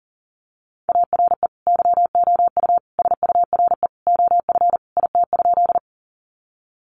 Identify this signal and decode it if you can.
Morse code, heard as a keyed tone, about 40 wpm, 700 Hz. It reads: ARE YOU SURE OF IT?
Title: Sentences from Top 300 Words 40wpm